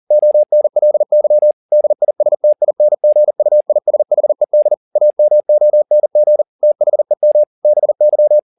Morse code, heard as a keyed tone, about 40 wpm, 600 Hz. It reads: ONLY DISTINGUISHED AMONG THEM BY